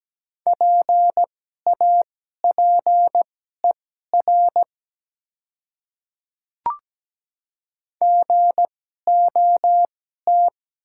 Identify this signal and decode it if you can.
Morse code, heard as a keyed tone, about 17 wpm, spaced as for 12 wpm, 700 Hz. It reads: PAPER GOT
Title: 2x Character Spacing - Top 300 Words - 17wpm